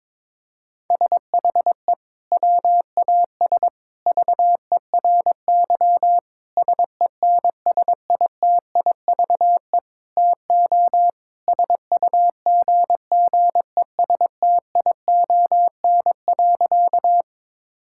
Morse code, heard as a keyed tone, about 22 wpm, 700 Hz. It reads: SHE WAS VERY SENSITIVE TO SUGGESTION.
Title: Sentences from Top 1600 Words 22wpm